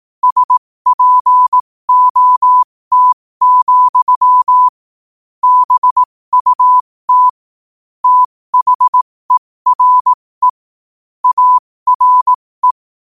Morse code, heard as a keyed tone, about 18 wpm, spaced as for 15 wpm, 1000 Hz. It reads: SPOT, BUT THERE ARE
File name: qotdfiles / 15wpm